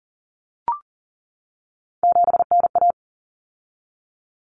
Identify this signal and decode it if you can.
Morse code, heard as a keyed tone, about 40 wpm, 700 Hz. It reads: 7NA
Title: Common Cut Numbers - 3-Digit 40wpm